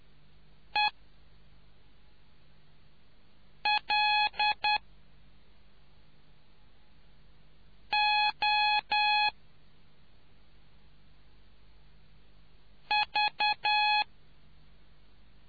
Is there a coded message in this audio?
Yes: Morse code from 0.7 to 14.0 s, heard as a keyed tone, about 10 wpm, 840 Hz. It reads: E L O V